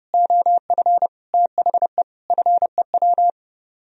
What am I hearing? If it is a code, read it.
Morse code, heard as a keyed tone, about 30 wpm, 700 Hz. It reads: OF THE FEW